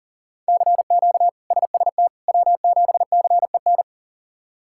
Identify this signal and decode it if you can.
Morse code, heard as a keyed tone, about 40 wpm, 700 Hz. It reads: CQ SST W7CEN
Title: K1USN SST QSO Practice - 40wpm